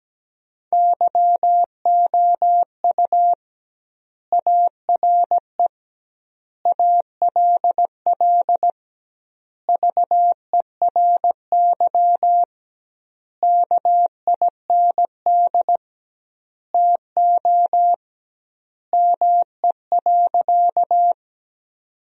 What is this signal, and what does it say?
Morse code, heard as a keyed tone, about 17 wpm, 700 Hz. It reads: YOU ARE ALL VERY KIND TO ME.
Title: 2x Word-Spacing - Sentences from Top 200 Words - 17wpm